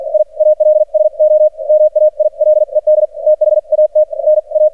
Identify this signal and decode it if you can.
Morse code, heard as a keyed tone, about 49 wpm, spaced as for 40 wpm, 610 Hz. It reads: I ALSO MAILED THAT NA
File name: archived / CRSnet